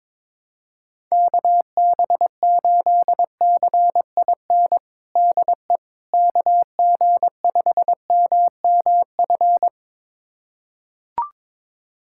Morse code, heard as a keyed tone, about 21 wpm, 700 Hz. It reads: KB8CIN DE KG5MMF
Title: USA Callsign DE Callsign - 6 Characters - 22wpm